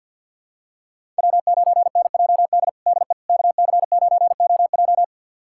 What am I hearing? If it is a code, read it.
Morse code, heard as a keyed tone, about 50 wpm, 700 Hz. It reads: W9NJD DE KC9OJ
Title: USA Callsign DE Callsign - 5 Characters - 50wpm